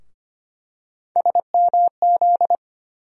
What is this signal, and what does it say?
Morse code, heard as a keyed tone, about 25 wpm, 700 Hz. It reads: SMZ
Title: USA Call Sign Postfixes - 3 Letters 25wpm